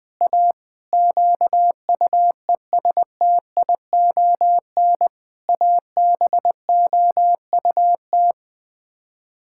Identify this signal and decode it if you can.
Morse code, heard as a keyed tone, about 20 wpm, 700 Hz. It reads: A QUESTION ABOUT
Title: Sets of 6 Words - 20wpm